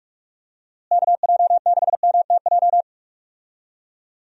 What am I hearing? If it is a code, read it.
Morse code, heard as a keyed tone, about 46 wpm, 700 Hz. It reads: KJ6MTJ
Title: USA Call Signs - 6 Characters - Repeated 3x - 45wpm